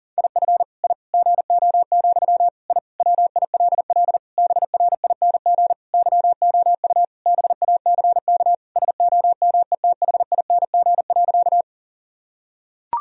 Morse code, heard as a keyed tone, about 37 wpm, 700 Hz. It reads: IF I GO, I WILL BRING YOU BACK SOMETHING.
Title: Sentences from Top 1000 Words - Review Part 3 - 40wpm